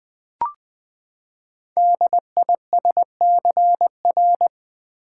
Morse code, heard as a keyed tone, about 20 wpm, 700 Hz. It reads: DISCR
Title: Build A Word - 5 Letters At A Time - 10-Letter Words - 20wpm